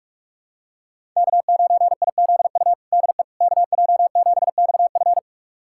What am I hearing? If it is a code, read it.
Morse code, heard as a keyed tone, about 41 wpm, 700 Hz. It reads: K9IZU DE KJ7XF
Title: USA Callsign DE Callsign - 5 Characters - 45wpm